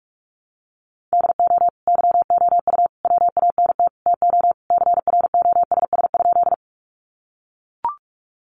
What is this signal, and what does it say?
Morse code, heard as a keyed tone, about 44 wpm, 700 Hz. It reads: DO YOU WANT TO CROSS?
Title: Sentences from Top 300 Words 45wpm